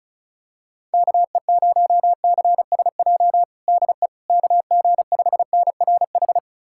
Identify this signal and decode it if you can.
Morse code, heard as a keyed tone, about 34 wpm, 700 Hz. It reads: KE0CSJ DE KG5NRH
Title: USA Callsign DE Callsign - 6 Characters - 35wpm